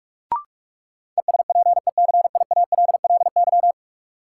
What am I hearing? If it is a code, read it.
Morse code, heard as a keyed tone, about 41 wpm, 700 Hz. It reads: ESPECIALLY
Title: Sentences from Top 700 Words - New Word Drill - 45wpm